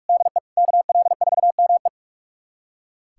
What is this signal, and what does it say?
Morse code, heard as a keyed tone, about 43 wpm, 690 Hz. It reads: DE KL4ME
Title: USA Callsign DE Callsign 45wpm